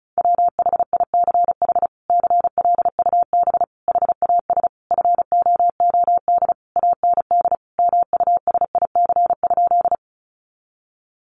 Morse code, heard as a keyed tone, about 34 wpm, 700 Hz. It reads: WHICH CLUB HAS FOOD AND MUSIC?
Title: Sentences from Top 1200 Words 35wpm